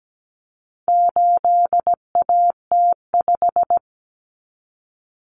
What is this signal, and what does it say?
Morse code, heard as a keyed tone, about 17 wpm, 700 Hz. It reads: 8AT5
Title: Common Cut Numbers - 4-Digit 17wpm